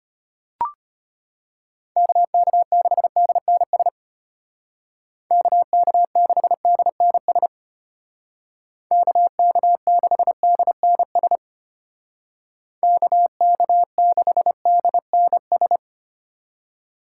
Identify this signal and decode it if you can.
Morse code, heard as a keyed tone, about 31 wpm, 700 Hz. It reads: KK6DNS KK6DNS KK6DNS KK6DNS
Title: Speed Racing - USA 2x3 Call Signs 25wpm